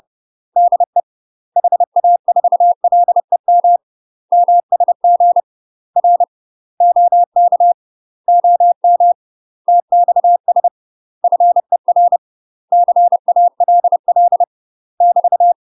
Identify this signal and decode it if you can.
Morse code, heard as a keyed tone, about 30 wpm, 700 Hz. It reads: DE HA4LEM MSG R OK OM TXS FER CALL <BT>
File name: UK MASTER QSOs TRAINING/30WPM